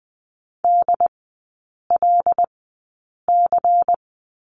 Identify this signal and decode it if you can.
Morse code, heard as a keyed tone, about 20 wpm, spaced as for 8 wpm, 700 Hz. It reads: DLC
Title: 133 Lesson 22e - Question Mark - Three Characters 20wpm